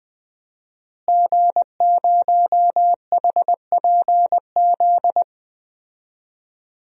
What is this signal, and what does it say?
Morse code, heard as a keyed tone, about 20 wpm, 700 Hz. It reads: G0HPZ
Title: DXCC-100 Call Signs - 5 Characters - Repeated 2x - 20wpm